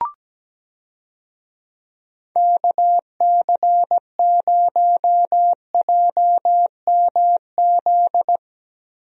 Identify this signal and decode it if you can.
Morse code, heard as a keyed tone, about 17 wpm, 700 Hz. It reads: KC0JMZ
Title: USA Call Signs - 6 Characters - 17wpm